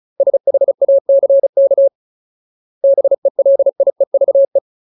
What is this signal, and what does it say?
Morse code, heard as a keyed tone, about 34 wpm, 550 Hz. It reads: SHACK BELIEVE
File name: Ultimate Word List / 35WPM Ultimate